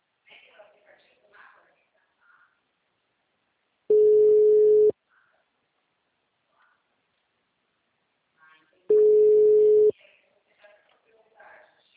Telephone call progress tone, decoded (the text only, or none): ringback tone